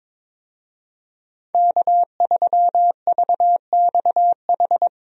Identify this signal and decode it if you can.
Morse code, heard as a keyed tone, about 22 wpm, 700 Hz. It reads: K3VXH
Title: USA Call Signs - 5 Characters - Repeated 2x - 22wpm